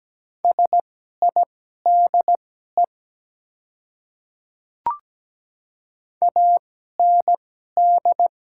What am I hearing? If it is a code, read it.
Morse code, heard as a keyed tone, about 17 wpm, spaced as for 12 wpm, 700 Hz. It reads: SIDE AND
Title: 2x Character Spacing - Top 200 Words - 17wpm